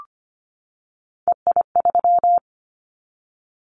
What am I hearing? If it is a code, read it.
Morse code, heard as a keyed tone, about 25 wpm, 700 Hz. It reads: EI3
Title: Callsign Prefixes for Top 50 DXCC - 25wpm